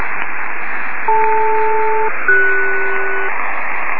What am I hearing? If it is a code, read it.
Aircraft SELCAL, heard as two pairs of tones. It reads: DL-CS
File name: 66web / hfselcal